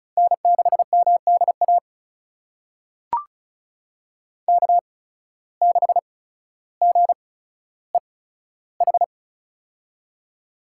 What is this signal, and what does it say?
Morse code, heard as a keyed tone, about 34 wpm, spaced as for 15 wpm, 700 Hz. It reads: N6MDA K6GEH
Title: USA Call Signs - 5 Characters - 8x Character Spacing - 35wpm